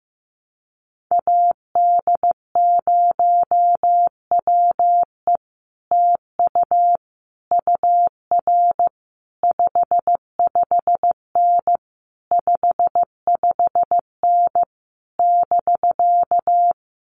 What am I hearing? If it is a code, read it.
Morse code, heard as a keyed tone, about 15 wpm, 700 Hz. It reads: AD0WE TU UR 55N 55N <BK>
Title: Common QSO Phrases Encore 15wpm